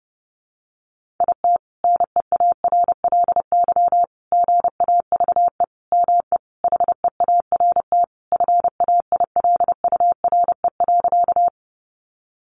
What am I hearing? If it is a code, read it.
Morse code, heard as a keyed tone, about 30 wpm, 700 Hz. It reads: IT NEARLY GAVE ME HEART FAILURE.
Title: Sentences from Top 1300 Words 30wpm